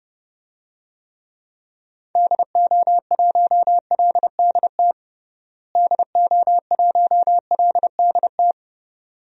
Morse code, heard as a keyed tone, about 32 wpm, 700 Hz. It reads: DO1LDT DO1LDT
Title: DXCC-100 Call Signs - 6 Characters - Repeated 2x - 30wpm